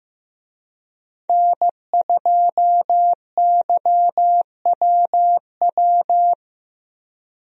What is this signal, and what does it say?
Morse code, heard as a keyed tone, about 15 wpm, 700 Hz. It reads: N2YWW
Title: USA Call Signs - 5 Characters - Repeated 2x - 15wpm